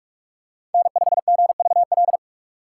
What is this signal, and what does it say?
Morse code, heard as a keyed tone, about 44 wpm, 700 Hz. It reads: N5GVL